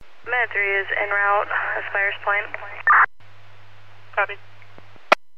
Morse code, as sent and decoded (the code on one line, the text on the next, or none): none